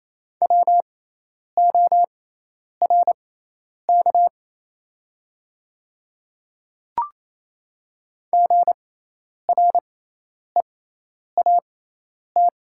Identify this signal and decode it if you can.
Morse code, heard as a keyed tone, about 28 wpm, spaced as for 10 wpm, 700 Hz. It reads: WORK GREAT
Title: 6x Character Spacing - Top 200 Words - 28wpm